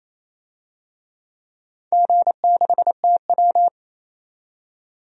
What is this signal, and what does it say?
Morse code, heard as a keyed tone, about 28 wpm, 700 Hz. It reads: G6TW
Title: DXCC-100 Call Signs - 4 Characters - 28wpm